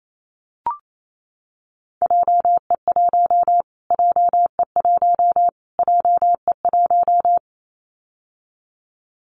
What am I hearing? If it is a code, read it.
Morse code, heard as a keyed tone, about 28 wpm, 700 Hz. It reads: JE1 JE1 JE1
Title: Callsign Prefixes for Top 25 DXCC - Repeated 3x - 28wpm